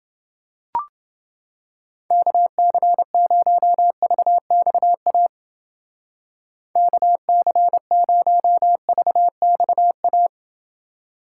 Morse code, heard as a keyed tone, about 28 wpm, 700 Hz. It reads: KC0VXA KC0VXA